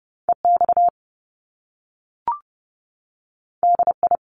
Morse code, heard as a keyed tone, about 30 wpm, 700 Hz. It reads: EX DI
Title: Top Two Letter Combinations in Words 30wpm